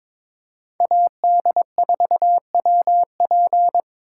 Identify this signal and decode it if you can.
Morse code, heard as a keyed tone, about 26 wpm, 700 Hz. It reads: AD4WP
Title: Speed Racing - USA 2x2 Call Signs 22wpm